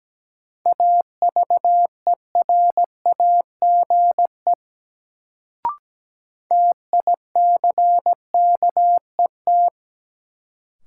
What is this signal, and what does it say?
Morse code, heard as a keyed tone, about 17 wpm, 700 Hz. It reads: AVERAGE TICKET
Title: Sentences from Top 1400 Words - New Word Drill Alt - 17wpm